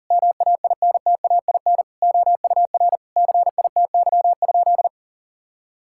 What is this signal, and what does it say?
Morse code, heard as a keyed tone, about 40 wpm, 700 Hz. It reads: MAINTAIN OUR CITY?
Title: Sentences from Top 1000 Words - Review Part 5 - 40wpm